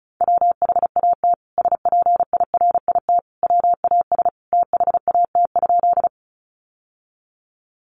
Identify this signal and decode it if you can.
Morse code, heard as a keyed tone, about 34 wpm, 700 Hz. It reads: WHAT SPIRIT WAS THAT?
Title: Sentences from Top 1400 Words 35wpm